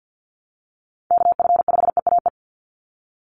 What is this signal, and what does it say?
Morse code, heard as a keyed tone, about 50 wpm, 700 Hz. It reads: KF5EAE
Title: USA Call Signs 50wpm